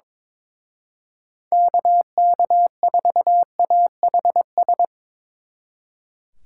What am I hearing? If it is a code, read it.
Morse code, heard as a keyed tone, about 24 wpm, 700 Hz. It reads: KK4AHS
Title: Speed Racing - USA Call Signs 22wpm